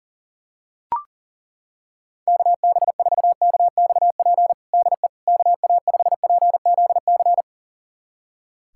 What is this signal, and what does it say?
Morse code, heard as a keyed tone, about 40 wpm, 700 Hz. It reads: KB4KXP DE KA5PZC
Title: USA Callsign DE Callsign - 6 Characters - 40wpm